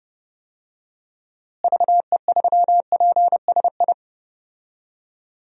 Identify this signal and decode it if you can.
Morse code, heard as a keyed tone, about 29 wpm, 700 Hz. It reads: VE3PSI
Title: DXCC-100 Call Signs - 6 Characters - Repeated 2x - 30wpm